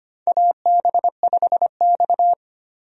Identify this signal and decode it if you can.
Morse code, heard as a keyed tone, about 25 wpm, 700 Hz. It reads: AB5X
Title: USA Call Signs - 4 Characters - 25wpm